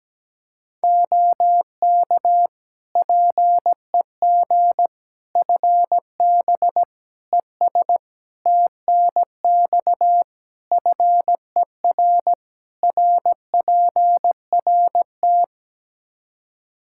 Morse code, heard as a keyed tone, about 17 wpm, 700 Hz. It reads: OK PEG FB ES TNX FER RPRT
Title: Practice QSO - Exchange 2 - Info - 17wpm